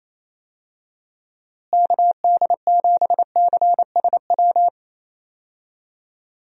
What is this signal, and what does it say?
Morse code, heard as a keyed tone, about 28 wpm, 700 Hz. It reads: KD7CSW
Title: USA Call Signs - 6 Characters - Repeated 3x - 28wpm